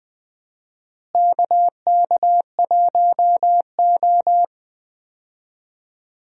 Morse code, heard as a keyed tone, about 20 wpm, 700 Hz. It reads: KK1O
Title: USA Call Signs - 4 Characters - Repeated 3x - 20wpm